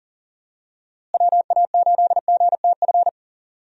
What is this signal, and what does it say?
Morse code, heard as a keyed tone, about 40 wpm, 700 Hz. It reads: WA8GTF
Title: USA Call Signs - 6 Characters - Repeated 3x - 40wpm